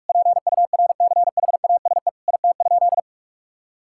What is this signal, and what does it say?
Morse code, heard as a keyed tone, about 41 wpm, 700 Hz. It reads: PURCHASE IT?
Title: Sentences from Top 1000 Words - Review Part 5 - 45wpm